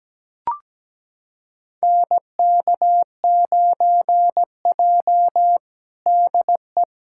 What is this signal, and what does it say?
Morse code, heard as a keyed tone, about 17 wpm, 700 Hz. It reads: NK9J DE
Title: USA Callsign DE Callsign - 4 Characters - 17wpm